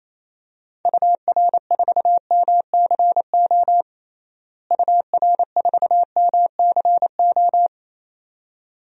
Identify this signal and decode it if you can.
Morse code, heard as a keyed tone, about 28 wpm, 700 Hz. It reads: UR4MCO UR4MCO
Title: DXCC-100 Call Signs - 6 Characters - Repeated 2x - 28wpm